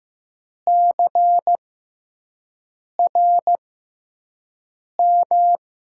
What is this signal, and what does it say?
Morse code, heard as a keyed tone, about 15 wpm, spaced as for 5 wpm, 700 Hz. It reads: CRM
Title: USA Call Signs - 6x Character Spacing - 15wpm